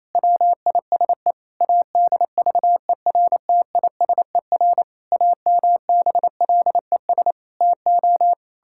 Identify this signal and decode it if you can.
Morse code, heard as a keyed tone, about 28 wpm, 700 Hz. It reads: WISE ADVERTISER AMBLES TO
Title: Pangram Challenge v2 - 28wpm